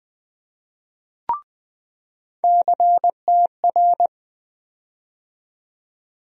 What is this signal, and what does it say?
Morse code, heard as a keyed tone, about 20 wpm, 700 Hz. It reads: CTR
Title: USA Call Sign Postfixes - 3 Letters 20wpm